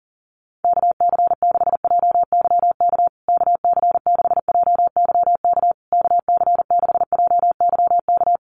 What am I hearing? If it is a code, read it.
Morse code, heard as a keyed tone, about 40 wpm, 700 Hz. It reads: KC6JYK KC6JYK KC6JYK
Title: USA Call Signs Repeated 3x 40wpm